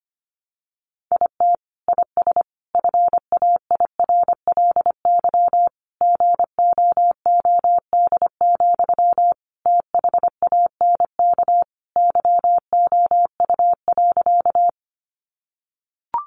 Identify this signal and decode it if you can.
Morse code, heard as a keyed tone, about 25 wpm, 700 Hz. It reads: IT IS FAIRLY GOOD, THANK YOU.
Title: Sentences from Top 1100 Words 25wpm